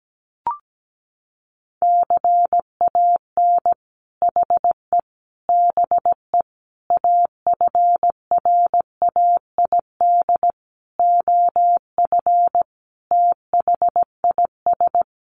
Morse code, heard as a keyed tone, about 17 wpm, 700 Hz. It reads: CAN HE BE AFRAID OF THIS
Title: Sentences from Top 1500 Words 17wpm